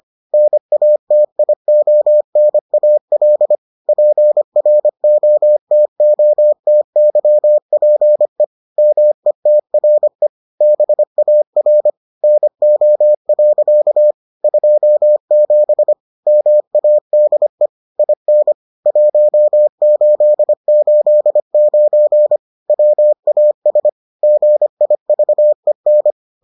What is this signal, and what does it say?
Morse code, heard as a keyed tone, about 25 wpm, 600 Hz. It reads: NATIONAL PROTOTYPE METRE BAR NO. 27 MADE IN 1889 WAS GIVEN